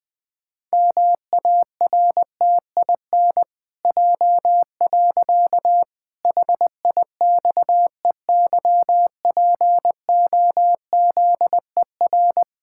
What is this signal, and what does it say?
Morse code, heard as a keyed tone, about 20 wpm, 700 Hz. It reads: MARTIN J. HIXEYPOZER
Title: Pangram Challenge v2 - 20wpm